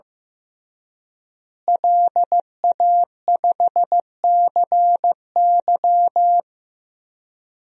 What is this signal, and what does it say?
Morse code, heard as a keyed tone, about 15 wpm, 700 Hz. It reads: LA5CY
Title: DXCC-100 Call Signs - 5 Characters - Repeated 3x - 15wpm